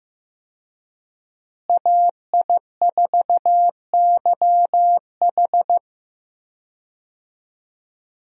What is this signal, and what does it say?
Morse code, heard as a keyed tone, about 15 wpm, 700 Hz. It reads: AI4YH
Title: USA Call Signs - 5 Characters - Repeated 2x - 15wpm